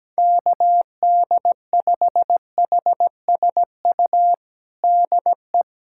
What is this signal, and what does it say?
Morse code, heard as a keyed tone, about 17 wpm, 700 Hz. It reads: KD5HSU DE
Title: USA Callsign DE Callsign - 6 Characters - 17wpm